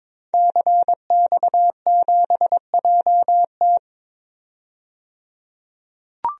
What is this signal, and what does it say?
Morse code, heard as a keyed tone, about 22 wpm, 700 Hz. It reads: CX7JT
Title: DXCC-100 Call Signs - 5 Characters - 22wpm